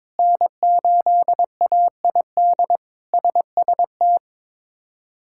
Morse code, heard as a keyed tone, about 22 wpm, 700 Hz. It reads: N8AID SST
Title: K1USN SST QSO Practice - 22wpm